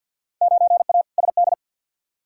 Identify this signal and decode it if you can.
Morse code, heard as a keyed tone, about 53 wpm, 700 Hz. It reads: 9A SD
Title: Field Day Exchanges Encore - 50wpm